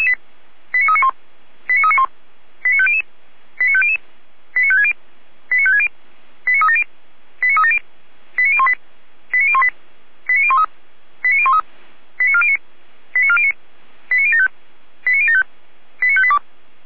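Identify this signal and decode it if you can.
Five-tone selective call (ZVEI), heard as quick runs of five tones, tones of about 70 ms. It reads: 89391 89391 89500 89500 89560 89560 89370 89370 89018 89018 89013 89013 89409 89409 89075 89075 89582
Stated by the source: ZVEI five-tone selective call received on an SDR.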